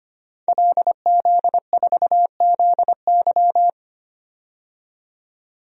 Morse code, heard as a keyed tone, about 25 wpm, 700 Hz. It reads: LZ4ZY
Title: DXCC-100 Call Signs - 5 Characters - 25wpm